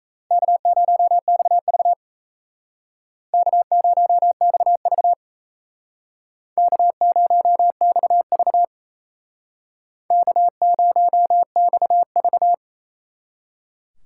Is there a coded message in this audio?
Yes: Morse code from 0.3 to 12.5 s, heard as a keyed tone, about 36 wpm, 700 Hz. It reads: K0XV K0XV K0XV K0XV